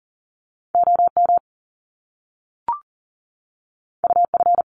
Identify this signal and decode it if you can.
Morse code, heard as a keyed tone, about 39 wpm, 700 Hz. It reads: OM UF
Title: Top Two Letter Combinations in Words 40wpm